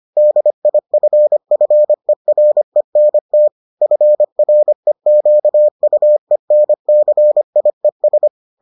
Morse code, heard as a keyed tone, about 25 wpm, 600 Hz. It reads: DIFFERENT FREQUENCIES